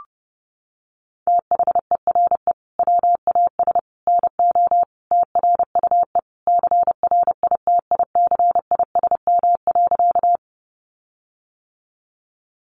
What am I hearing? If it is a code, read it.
Morse code, heard as a keyed tone, about 30 wpm, 700 Hz. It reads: THERE WAS NO TRUE CRITICISM.
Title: Sentences from Top 1300 Words 30wpm